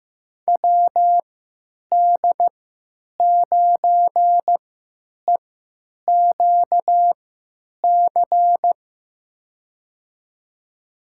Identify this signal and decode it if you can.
Morse code, heard as a keyed tone, about 15 wpm, spaced as for 9 wpm, 700 Hz. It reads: WD9EQC
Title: USA Call Signs - 6 Characters - 3x Character Spacing - 15wpm